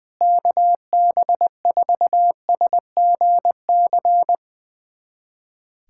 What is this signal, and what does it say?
Morse code, heard as a keyed tone, about 20 wpm, 700 Hz. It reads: KB4SGC
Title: USA Callsign DE Callsign - 6 Characters - 20wpm